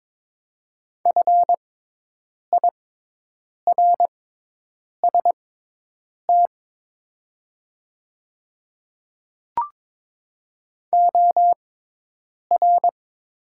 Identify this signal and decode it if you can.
Morse code, heard as a keyed tone, about 22 wpm, spaced as for 8 wpm, 700 Hz. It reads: FIRST OR
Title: 6x Character Spacing - Top 100 Words - 22wpm